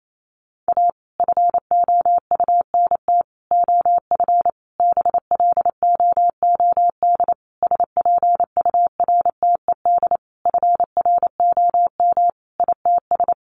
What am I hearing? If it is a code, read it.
Morse code, heard as a keyed tone, about 28 wpm, 700 Hz. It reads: A FOUNT OF BLOOD SPURTED FROM ITS
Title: A Sound Of Thunder 28wpm